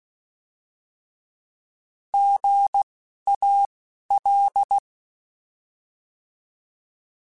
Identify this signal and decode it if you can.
Morse code, heard as a keyed tone, about 16 wpm, 800 Hz. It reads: G A L
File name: Misc / Misc-mp3s / Misc @10